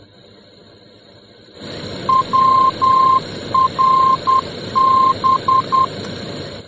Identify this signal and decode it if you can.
Morse code, heard as a keyed tone, about 10 wpm, 1050 Hz, with noise. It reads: WRB